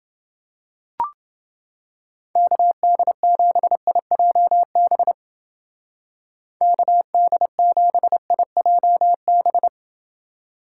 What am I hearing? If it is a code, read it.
Morse code, heard as a keyed tone, about 29 wpm, 700 Hz. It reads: KD7IJB KD7IJB